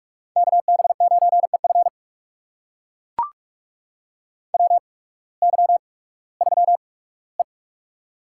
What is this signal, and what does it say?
Morse code, heard as a keyed tone, about 44 wpm, spaced as for 21 wpm, 700 Hz. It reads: KB9EF WY3E
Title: USA Call Signs - 8x Character Spacing - 45wpm